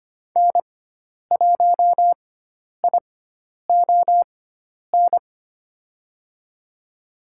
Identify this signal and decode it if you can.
Morse code, heard as a keyed tone, about 25 wpm, spaced as for 10 wpm, 700 Hz. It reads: N1ION
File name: Lesson 16g - Letter c - Call Signs 25wpm